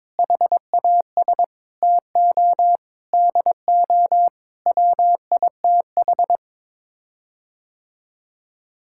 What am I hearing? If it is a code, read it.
Morse code, heard as a keyed tone, about 22 wpm, 700 Hz. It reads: HAS TO DO WITH
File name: practice / 22wpm / Sets of 8 Words - 22wpm